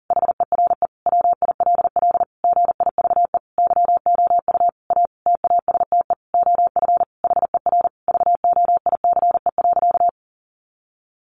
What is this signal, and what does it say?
Morse code, heard as a keyed tone, about 40 wpm, 700 Hz. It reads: HERE WILL GIVE YOU A TASTE OF HER VOICE.
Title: Sentences from Top 1100 Words 40wpm